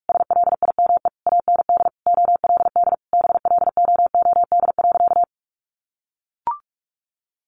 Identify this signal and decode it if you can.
Morse code, heard as a keyed tone, about 44 wpm, 700 Hz. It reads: SLIME AND OLD BLOOD.
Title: A Sound Of Thunder 45wpm